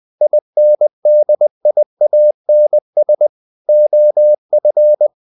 Morse code, heard as a keyed tone, about 20 wpm, 600 Hz. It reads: INDIANS OF
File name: the old man and the sea / the old man and the sea part 1 20wpm